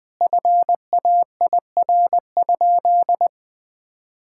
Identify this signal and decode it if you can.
Morse code, heard as a keyed tone, about 21 wpm, 700 Hz. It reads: FAIR?
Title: Sentences with 4-Word Phrases - 20wpm